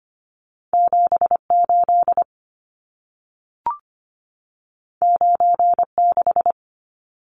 Morse code, heard as a keyed tone, about 25 wpm, 700 Hz. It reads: 78 96
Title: Numbers - 2-Digit 25wpm